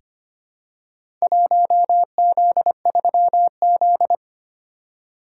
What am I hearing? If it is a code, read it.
Morse code, heard as a keyed tone, about 25 wpm, 700 Hz. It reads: 1Z3Z
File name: practice / 25wpm / Four Letter-Number Combinations - 25wpm